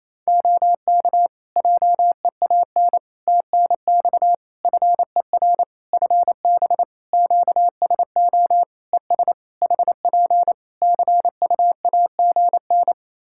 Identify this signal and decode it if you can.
Morse code, heard as a keyed tone, about 28 wpm, 700 Hz. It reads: OK JEAN TNX FER FB QSO ES HP CUAGN
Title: Practice QSO - Exchange 4 - Ending - 28wpm